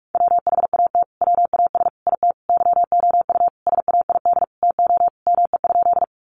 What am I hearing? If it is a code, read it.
Morse code, heard as a keyed tone, about 44 wpm, 700 Hz. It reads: WHAT WAS IT YOU SAID TO ME?
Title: Sentences from Top 200 Words 45wpm